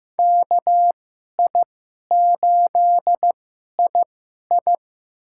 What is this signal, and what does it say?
Morse code, heard as a keyed tone, about 15 wpm, spaced as for 11 wpm, 700 Hz. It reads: KI8II
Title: USA Call Signs - 5 Characters - 2x Character Spacing - 15wpm